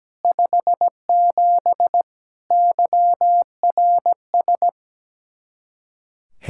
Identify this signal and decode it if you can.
Morse code, heard as a keyed tone, about 17 wpm, 700 Hz. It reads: 57 YRS
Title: K1USN SST QSO Practice 17wpm